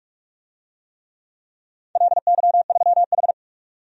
Morse code, heard as a keyed tone, about 46 wpm, 700 Hz. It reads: LY3H
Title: DXCC-100 Call Signs - 4 Characters - 45wpm